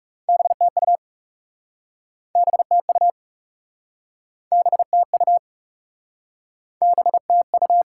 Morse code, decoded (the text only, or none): BTU BTU BTU BTU